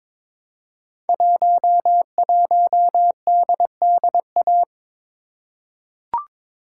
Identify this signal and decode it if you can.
Morse code, heard as a keyed tone, about 22 wpm, 700 Hz. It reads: 11DDA
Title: Code Groups - 5 Letter:Numbers - 22wpm